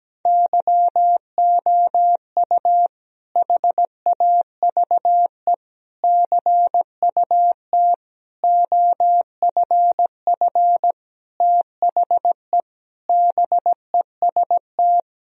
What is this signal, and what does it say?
Morse code, heard as a keyed tone, about 17 wpm, 700 Hz. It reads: YOU HAVE CUT OFF THE BEST
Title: Sentences from Top 1000 Words - Review Part 1 - 17wpm